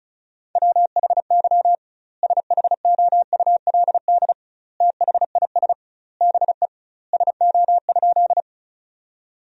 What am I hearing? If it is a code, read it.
Morse code, heard as a keyed tone, about 36 wpm, 700 Hz. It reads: WHY SHOULD THIS BE SO?
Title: 2x Word-Spacing - Sentences from Top 300 Words - 35wpm